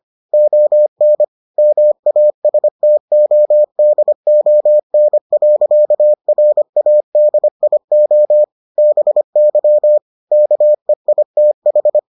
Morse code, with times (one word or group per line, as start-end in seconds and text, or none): 0.3-1.2 ON
1.6-8.4 MASTODON.RADIO
8.8-10.0 BY
10.3-12.0 KEITH